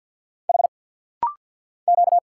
Morse code, heard as a keyed tone, about 47 wpm, 700 Hz. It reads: H Q
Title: Single Letters - Rapid-Fire 50wpm